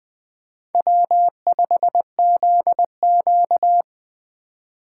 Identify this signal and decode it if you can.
Morse code, heard as a keyed tone, about 20 wpm, 700 Hz. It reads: W5ZQ
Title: USA Call Signs - 4 Characters - 2x Character Spacing - 20wpm